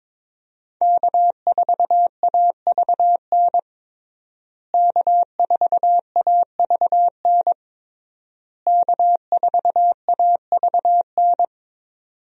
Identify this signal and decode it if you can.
Morse code, heard as a keyed tone, about 21 wpm, 700 Hz. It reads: K4AVN K4AVN K4AVN
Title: USA Call Signs - 5 Characters - Repeated 3x - 22wpm